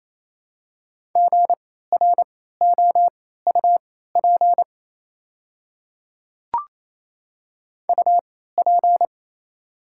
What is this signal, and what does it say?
Morse code, heard as a keyed tone, about 28 wpm, spaced as for 16 wpm, 690 Hz. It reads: GROUP UP
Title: 3x Character Spacing - Top 500 Words - 28wpm